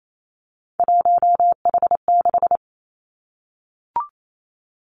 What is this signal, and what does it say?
Morse code, heard as a keyed tone, about 28 wpm, 700 Hz. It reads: 1H6
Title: Three Challenging Letter-Number Combinations 28wpm